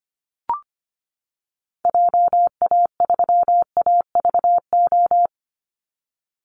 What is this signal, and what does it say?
Morse code, heard as a keyed tone, about 25 wpm, 700 Hz. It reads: JA3AVO
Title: Call Signs - 25wpm